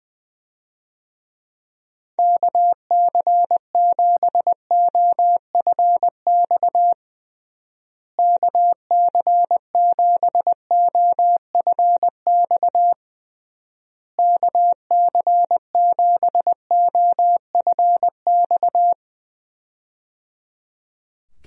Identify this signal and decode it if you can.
Morse code, heard as a keyed tone, about 21 wpm, 700 Hz. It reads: KC7OFX KC7OFX KC7OFX
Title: USA Call Signs - 6 Characters - Repeated 3x - 20wpm